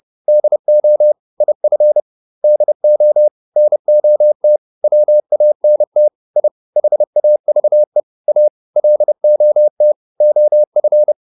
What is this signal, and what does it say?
Morse code, heard as a keyed tone, about 30 wpm, 600 Hz. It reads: DO IF DO NOT WANT I HAVE A LOT OF